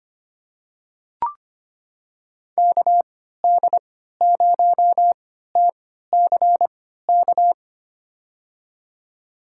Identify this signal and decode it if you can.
Morse code, heard as a keyed tone, about 25 wpm, spaced as for 14 wpm, 700 Hz. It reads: KD0TCK
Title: USA Call Signs - 3x Character Spacing - 25wpm